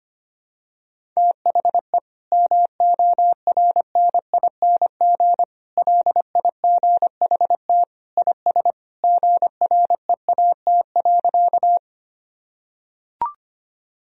Morse code, heard as a keyed tone, about 25 wpm, 700 Hz. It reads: THE MORNING LIGHT IS GREAT.
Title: Sentences from Top 1000 Words - Review Part 1 - 25wpm